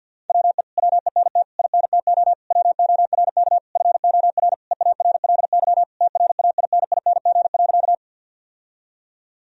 Morse code, heard as a keyed tone, about 50 wpm, 700 Hz. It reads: WE WENT INTO WORK FOR EARLY TRAINING.
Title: Sentences from Top 1000 Words - Review Part 4 - 50wpm